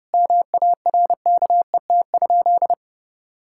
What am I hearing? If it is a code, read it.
Morse code, heard as a keyed tone, about 30 wpm, 700 Hz. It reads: MARKET?
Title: Sentences from Top 1000 Words - Review Part 6 - 30wpm